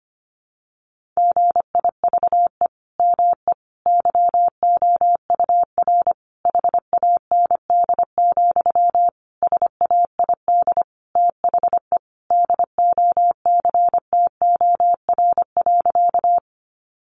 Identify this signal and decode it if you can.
Morse code, heard as a keyed tone, about 25 wpm, 700 Hz. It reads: GIVE ME YOUR HAND, SAID THE DOCTOR.
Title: Sentences from Top 1500 Words 25wpm